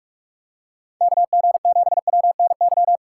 Morse code, heard as a keyed tone, about 43 wpm, 700 Hz. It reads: KG7WNY
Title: USA Call Signs - 6x Character Spacing - 45wpm